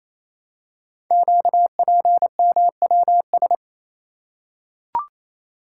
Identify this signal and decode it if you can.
Morse code, heard as a keyed tone, about 27 wpm, 700 Hz. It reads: QPMWS
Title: Code Groups - 5 Letters - 28wpm